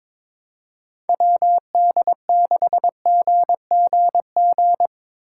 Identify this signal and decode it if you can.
Morse code, heard as a keyed tone, about 23 wpm, 700 Hz. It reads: WD6GGG
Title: USA Call Signs - 6 Characters - 2x Character Spacing - 22wpm